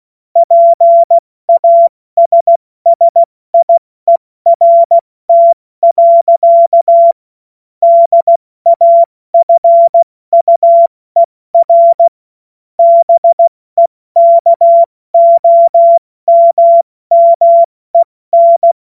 Morse code, heard as a keyed tone, about 16 wpm, 680 Hz. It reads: PASSIERT. DAFUER BEKOMMEN
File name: Morsekurs / Witze / Witz 28. Dezember 2021, 70 F80